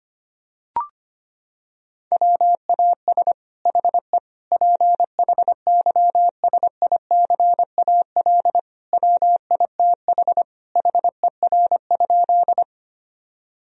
Morse code, heard as a keyed tone, about 25 wpm, 700 Hz. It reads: WAS HE PHYSICAL WITH HER?
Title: Sentences from Top 1000 Words - Review Part 4 - 25wpm